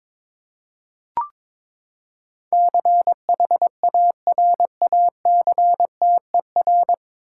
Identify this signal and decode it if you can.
Morse code, heard as a keyed tone, about 21 wpm, 700 Hz. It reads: CHARACTER
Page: Morse Code Ninja practice 22wpm